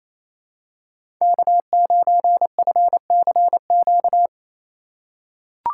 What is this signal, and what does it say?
Morse code, heard as a keyed tone, about 28 wpm, 700 Hz. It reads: K9FCQ
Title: USA Call Signs - 5 Characters - 2x Character Spacing - 28wpm